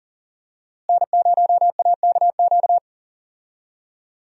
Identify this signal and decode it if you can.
Morse code, heard as a keyed tone, about 40 wpm, 700 Hz. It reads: N0AKQ
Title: USA Call Signs - 5 Characters - Repeated 3x - 40wpm